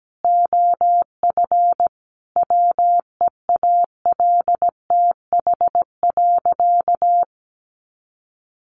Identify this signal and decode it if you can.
Morse code, heard as a keyed tone, about 17 wpm, 700 Hz. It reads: OF WEALTH.
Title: Sentences from Top 1200 Words 17wpm